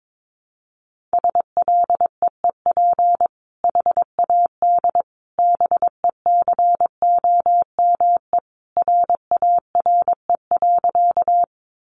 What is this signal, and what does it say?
Morse code, heard as a keyed tone, about 22 wpm, 700 Hz. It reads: SLEEP HAD BECOME RARE.
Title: Sentences from Top 1100 Words 22wpm